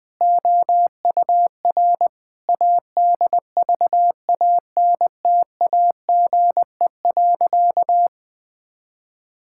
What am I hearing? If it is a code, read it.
Morse code, heard as a keyed tone, about 21 wpm, 700 Hz. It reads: OUR ADVANTAGE.
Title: Sentences from Top 1000 Words - Review Part 6 - 20wpm